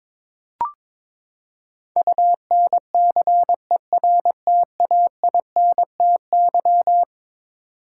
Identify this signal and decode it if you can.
Morse code, heard as a keyed tone, about 22 wpm, 700 Hz. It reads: UNCERTAINTY
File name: practice / 22wpm / Eleven Letter Words - 22wpm